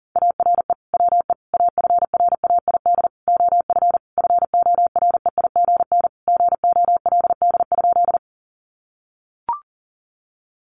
Morse code, heard as a keyed tone, about 40 wpm, 700 Hz. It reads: ARE WE AFRAID OF FOREIGN GOLD?
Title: Sentences from Top 1500 Words 40wpm